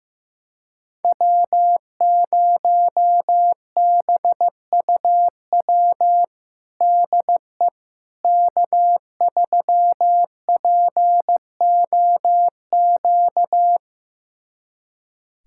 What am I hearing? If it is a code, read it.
Morse code, heard as a keyed tone, about 15 wpm, 700 Hz. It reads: W0BUW DE K3POQ
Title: USA Callsign DE Callsign - 5 Characters - 15wpm